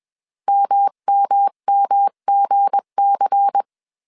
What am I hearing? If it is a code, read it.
Morse code, heard as a keyed tone, about 21 wpm, 800 Hz. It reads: MMMGC
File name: CW / DET / 1 9 1500C S B 20wpm 2313